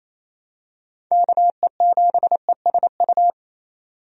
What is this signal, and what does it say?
Morse code, heard as a keyed tone, about 28 wpm, 700 Hz. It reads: KE7ESU
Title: USA Call Signs - 6 Characters - 28wpm